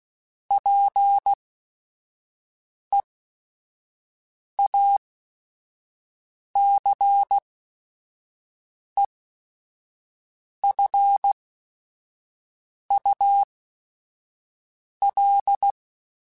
Morse code, heard as a keyed tone, about 16 wpm, spaced as for 5 wpm, 800 Hz. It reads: PEACEFUL
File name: Quotes / Quotes-mp3s / ChildAsleep @5